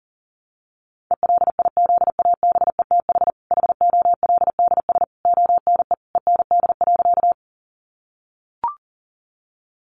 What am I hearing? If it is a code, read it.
Morse code, heard as a keyed tone, about 37 wpm, 700 Hz. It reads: ELIZABETH HOLDS ONE END.
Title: Easy Phrases 40wpm